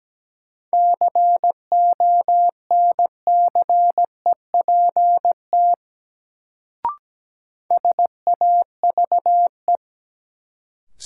Morse code, heard as a keyed tone, about 17 wpm, 700 Hz. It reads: CONCEPT SAVE
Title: Sentences from Top 1100 Words - New Word Drill - 17wpm